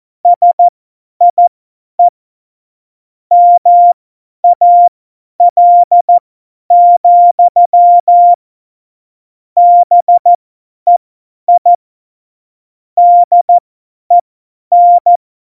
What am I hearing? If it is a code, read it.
Morse code, heard as a keyed tone, about 14 wpm, spaced as for 10 wpm, 700 Hz. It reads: SIE MAL, BEI DEN
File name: Morsekurs / Witze / Witz 22. Oktober 2019, 50 F70